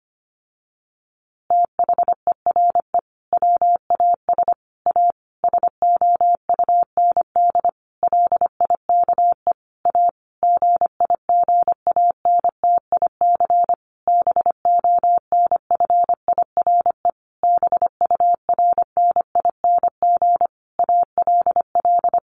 Morse code, heard as a keyed tone, about 25 wpm, 700 Hz. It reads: THERE WAS A SOUND LIKE A GIGANTIC BONFIRE BURNING ALL
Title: A Sound Of Thunder 25wpm